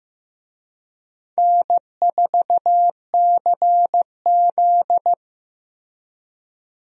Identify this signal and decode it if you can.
Morse code, heard as a keyed tone, about 16 wpm, 700 Hz. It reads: N4CZ
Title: USA Call Signs - 4 Characters - 15wpm